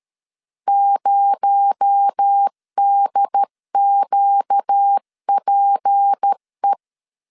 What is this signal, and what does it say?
Morse code, heard as a keyed tone, about 13 wpm, 800 Hz. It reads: 0DQPE
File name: CW / DET / 1 6 1500C S C 12wpm 3533